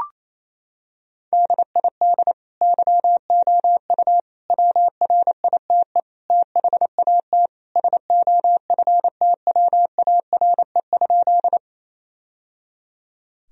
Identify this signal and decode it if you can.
Morse code, heard as a keyed tone, about 28 wpm, 700 Hz. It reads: DID YOU WRITE THAT SOFTWARE?
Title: Sentences from Top 1000 Words - Review Part 3 - 28wpm